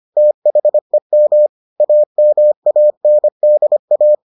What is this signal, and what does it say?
Morse code, heard as a keyed tone, about 25 wpm, 600 Hz. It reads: THEM AMANDA